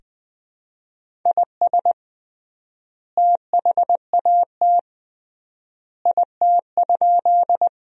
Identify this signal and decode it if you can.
Morse code, heard as a keyed tone, about 19 wpm, 700 Hz. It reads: IS THAT IT?
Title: 3x Word-Spacing - Sentences from Top 100 Words - 20wpm